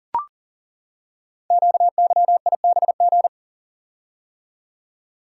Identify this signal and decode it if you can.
Morse code, heard as a keyed tone, about 41 wpm, 700 Hz. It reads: QYIBG
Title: Code Groups - 5 Letters - 40wpm